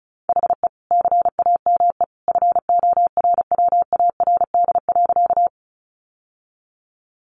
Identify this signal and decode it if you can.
Morse code, heard as a keyed tone, about 34 wpm, 700 Hz. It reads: HE CAME FORWARD.
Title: Sentences from Top 1300 Words 35wpm